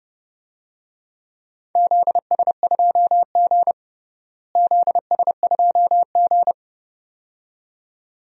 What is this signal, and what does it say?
Morse code, heard as a keyed tone, about 30 wpm, 700 Hz. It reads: ZS2G ZS2G
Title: DXCC-100 Call Signs - 4 Characters - Repeated 2x - 30wpm